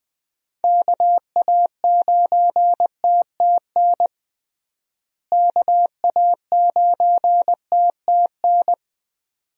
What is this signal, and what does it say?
Morse code, heard as a keyed tone, about 20 wpm, 700 Hz. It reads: KA9TTN KA9TTN
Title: USA Call Signs - 6 Characters - Repeated 2x - 20wpm